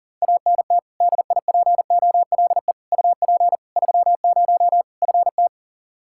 Morse code, heard as a keyed tone, about 40 wpm, 700 Hz. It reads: ANT DIPOLE UP 30 FT
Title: K1USN SST QSO Practice 40wpm